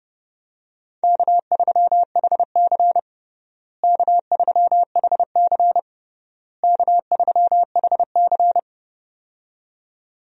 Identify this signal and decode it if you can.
Morse code, heard as a keyed tone, about 30 wpm, 700 Hz. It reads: K3HC K3HC K3HC
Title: USA Call Signs - 4 Characters - Repeated 3x - 30wpm